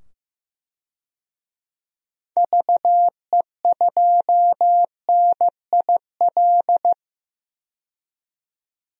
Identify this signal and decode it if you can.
Morse code, heard as a keyed tone, about 15 wpm, 700 Hz. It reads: VE2NIL
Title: DXCC-100 Call Signs - 6 Characters - 15wpm